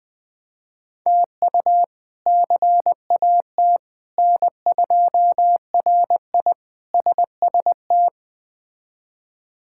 Morse code, heard as a keyed tone, about 20 wpm, 700 Hz. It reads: TU CAT N2RI SST
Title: K1USN SST QSO Practice - 20wpm